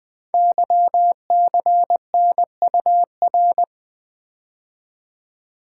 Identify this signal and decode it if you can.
Morse code, heard as a keyed tone, about 20 wpm, 700 Hz. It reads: YCNUR
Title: Code Groups - 5 Letter:Numbers - 20wpm